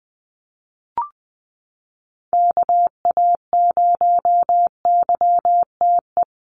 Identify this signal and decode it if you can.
Morse code, heard as a keyed tone, about 20 wpm, 700 Hz. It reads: KA0YTE
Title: USA Call Signs Repeated 2x 20wpm